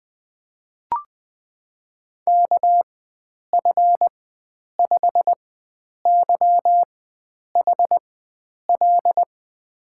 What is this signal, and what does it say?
Morse code, heard as a keyed tone, about 20 wpm, spaced as for 9 wpm, 700 Hz. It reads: KF5YHL
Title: USA Call Signs - 6 Characters - 4x Character Spacing - 20wpm